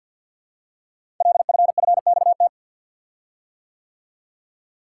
Morse code, heard as a keyed tone, about 50 wpm, 700 Hz. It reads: LFFXT
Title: Code Groups - 5 Letters - 50wpm